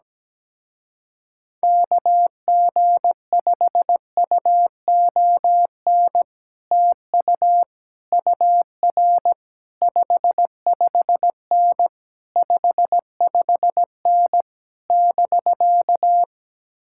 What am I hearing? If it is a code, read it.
Morse code, heard as a keyed tone, about 17 wpm, 700 Hz. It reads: KG5UON TU UR 55N 55N <BK>
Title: Speed-Racing Common QSO Phrases Encore - 17wpm